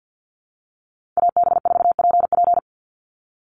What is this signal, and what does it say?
Morse code, heard as a keyed tone, about 50 wpm, 700 Hz. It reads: AB4PP
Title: NA SOTA Chaser Callsigns 50wpm